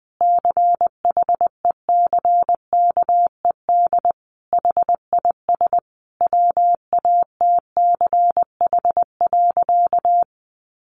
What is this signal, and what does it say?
Morse code, heard as a keyed tone, about 20 wpm, 700 Hz. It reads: CHECKED HIS WATCH.
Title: A Sound Of Thunder 20wpm